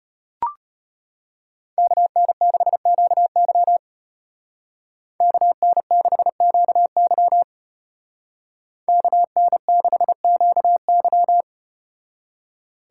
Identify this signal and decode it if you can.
Morse code, heard as a keyed tone, about 34 wpm, 700 Hz. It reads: KN6QY KN6QY KN6QY